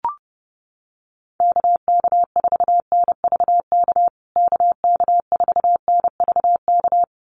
Morse code, heard as a keyed tone, about 30 wpm, 700 Hz. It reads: KK4NVK KK4NVK
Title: USA Call Signs Repeated 2x 30wpm